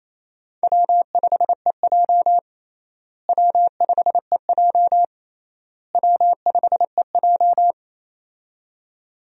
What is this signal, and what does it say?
Morse code, heard as a keyed tone, about 28 wpm, 700 Hz. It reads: W5EJ W5EJ W5EJ
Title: USA Call Signs - 4 Characters - Repeated 3x - 28wpm